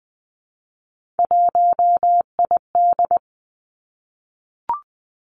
Morse code, heard as a keyed tone, about 20 wpm, 700 Hz. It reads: 1ID
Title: Three Challenging Letter-Number Combinations 20wpm